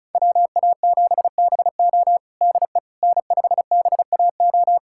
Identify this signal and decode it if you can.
Morse code, heard as a keyed tone, about 36 wpm, 700 Hz. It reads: WA7BO DE N5BAO
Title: USA Callsign DE Callsign - 5 Characters - 35wpm